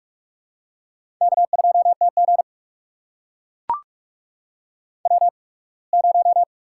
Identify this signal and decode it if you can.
Morse code, heard as a keyed tone, about 44 wpm, spaced as for 27 wpm, 690 Hz. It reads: K2TG W0
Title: USA Call Signs - 4 Characters - 8x Character Spacing - 45wpm